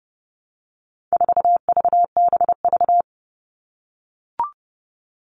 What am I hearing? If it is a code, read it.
Morse code, heard as a keyed tone, about 30 wpm, 700 Hz. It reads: 4VBV
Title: All Cut Numbers - 4-Digit 30wpm